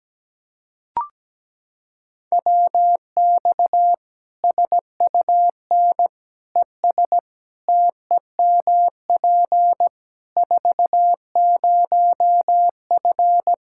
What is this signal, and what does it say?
Morse code, heard as a keyed tone, about 17 wpm, 700 Hz. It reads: WX SUN ES TEMP 40F
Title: Practice QSO - Exchange 2 - Info - 17wpm